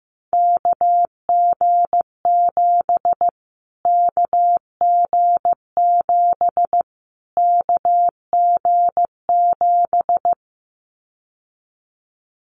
Morse code, heard as a keyed tone, about 15 wpm, 700 Hz. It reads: KG7 KG7 KG7
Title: Callsign Prefixes for Top 25 DXCC - Repeated 3x - 15wpm